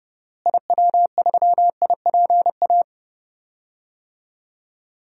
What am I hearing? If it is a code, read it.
Morse code, heard as a keyed tone, about 30 wpm, 700 Hz. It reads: IW3IPA
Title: DXCC-100 Call Signs - 6 Characters - Repeated 3x - 30wpm